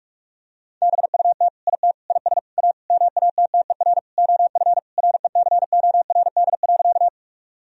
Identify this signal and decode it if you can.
Morse code, heard as a keyed tone, about 46 wpm, 700 Hz. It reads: BUT IT IS A MATTER OF RECORD.
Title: Sentences from Top 1000 Words - Review Part 3 - 45wpm